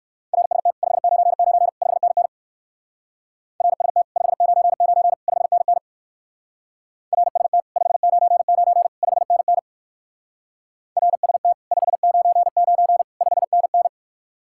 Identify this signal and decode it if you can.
Morse code, heard as a keyed tone, about 55 wpm, 700 Hz. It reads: RST 599 5NN RST 599 5NN RST 599 5NN RST 599 5NN
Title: Speed-Racing Common QSO Phrases Encore - 45wpm